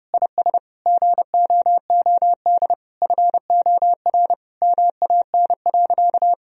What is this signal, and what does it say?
Morse code, heard as a keyed tone, about 30 wpm, 700 Hz. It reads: IS GOOD FOR MAN.
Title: Sentences from Top 1000 Words - Review Part 3 - 30wpm